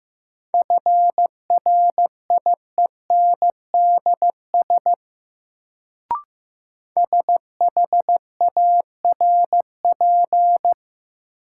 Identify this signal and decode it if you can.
Morse code, heard as a keyed tone, about 15 wpm, 700 Hz. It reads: FRIENDS SHARP
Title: Sentences from Top 1600 Words - New Word Drill Alt - 15wpm